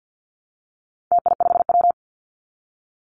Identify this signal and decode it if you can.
Morse code, heard as a keyed tone, about 50 wpm, 700 Hz. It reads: TI5W
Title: Call Signs Repeated 2x - 50wpm